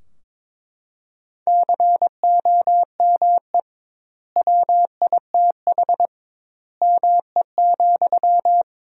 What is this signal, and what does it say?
Morse code, heard as a keyed tone, about 22 wpm, 700 Hz. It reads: COME WITH ME,
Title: 2x Word-Spacing - Sentences from Top 200 Words - 22wpm